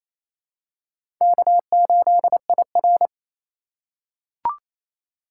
Morse code, heard as a keyed tone, about 28 wpm, 700 Hz. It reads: K8IR
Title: USA Call Signs - 4 Characters - 2x Character Spacing - 28wpm